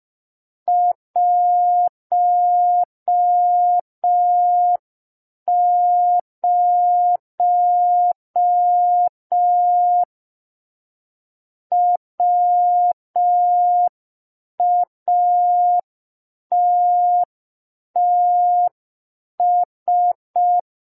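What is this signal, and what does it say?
Morse code, heard as a keyed tone, about 5 wpm, 710 Hz. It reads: 10 WATTS
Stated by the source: Morse receiving practice lesson for an amateur radio Foundation assessment.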